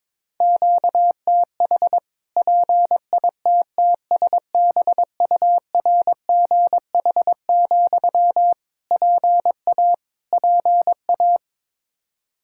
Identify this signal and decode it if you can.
Morse code, heard as a keyed tone, about 21 wpm, 700 Hz. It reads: QTH PITTSBURGH, PA PA
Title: Practice QSO - Exchange 1 - Beginning - 22wpm